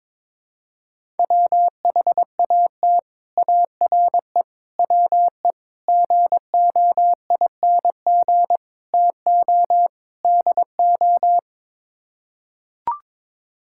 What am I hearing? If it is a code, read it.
Morse code, heard as a keyed tone, about 22 wpm, 700 Hz. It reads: WHAT ARE WE GOING TO DO
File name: practice / 22wpm / Sets of 6 Words - 22wpm